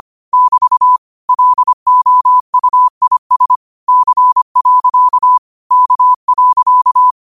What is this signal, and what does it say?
Morse code, heard as a keyed tone, about 25 wpm, 1000 Hz. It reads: <BT> LOUIS C. K.
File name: 25wpm